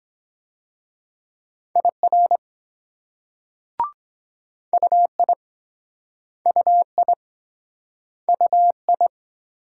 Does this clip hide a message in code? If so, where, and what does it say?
Yes: Morse code from 1.8 to 9.1 s, heard as a keyed tone, about 22 wpm, 700 Hz. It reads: IR UI UI UI